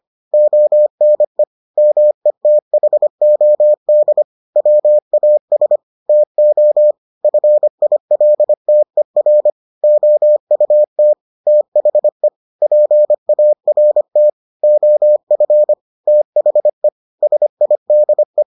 Morse code, heard as a keyed tone, about 25 wpm, 600 Hz. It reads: ONE METHOD WAS TO FILTER OUT THE PART OF THE SIDE